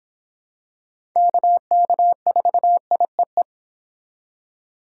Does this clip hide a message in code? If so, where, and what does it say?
Yes: Morse code from 1.2 to 3.4 s, heard as a keyed tone, about 25 wpm, 700 Hz. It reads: KK4IEE